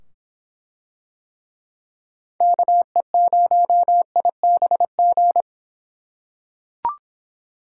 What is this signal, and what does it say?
Morse code, heard as a keyed tone, about 24 wpm, 700 Hz. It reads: KE0IBG